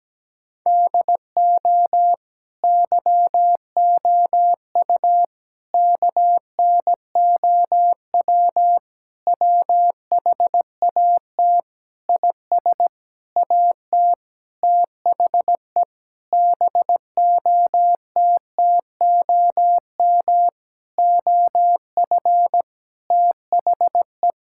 Morse code, heard as a keyed tone, about 17 wpm, 700 Hz. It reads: DO YOU KNOW WHAT IS AT THE BOTTOM OF THE
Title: Sentences from Top 1000 Words - Review Part 6 - 17wpm